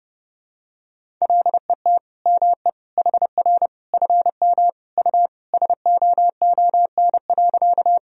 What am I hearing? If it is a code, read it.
Morse code, heard as a keyed tone, about 30 wpm, 700 Hz. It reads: LET ME HR FM U SOON.
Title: Sentences from Top 400 Words and Abbreviations 30wpm